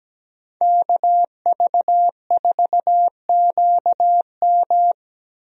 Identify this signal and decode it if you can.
Morse code, heard as a keyed tone, about 17 wpm, 700 Hz. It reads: KV4QM
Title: USA Call Signs - 5 Characters - Repeated 2x - 17wpm